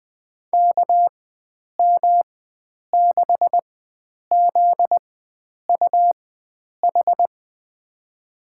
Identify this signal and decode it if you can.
Morse code, heard as a keyed tone, about 21 wpm, spaced as for 9 wpm, 700 Hz. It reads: KM6ZUH
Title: USA Call Signs - 6 Characters - 4x Character Spacing - 20wpm